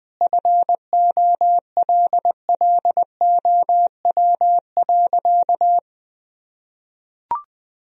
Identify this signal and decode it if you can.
Morse code, heard as a keyed tone, about 20 wpm, 700 Hz. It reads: FOLLOW.
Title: Sentences from Top 1000 Words - Review Part 3 - 20wpm